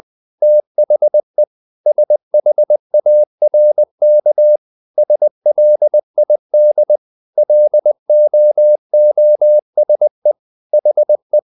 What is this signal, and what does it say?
Morse code, heard as a keyed tone, about 20 wpm, 600 Hz. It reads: THE SHARK SLID LOOSE HE
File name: the old man and the sea part 2 20wpm